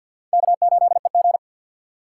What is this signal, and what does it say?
Morse code, heard as a keyed tone, about 53 wpm, 690 Hz. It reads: K8EG
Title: USA Call Signs - 4 Characters - 6x Character Spacing - 50wpm